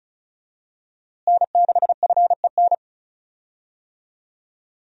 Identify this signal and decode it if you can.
Morse code, heard as a keyed tone, about 36 wpm, 700 Hz. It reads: N6FEN
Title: USA Call Signs - 5 Characters - Repeated 3x - 35wpm